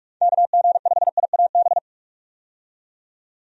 Morse code, heard as a keyed tone, about 46 wpm, 700 Hz. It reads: KG5IAB
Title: USA Call Signs - 6 Characters - 45wpm